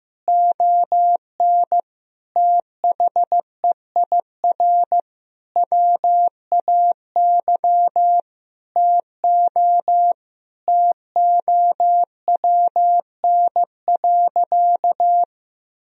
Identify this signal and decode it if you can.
Morse code, heard as a keyed tone, about 15 wpm, 700 Hz. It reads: ON THEIR WAY TO TOWN.
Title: Sentences from Top 1000 Words - Review Part 1 - 15wpm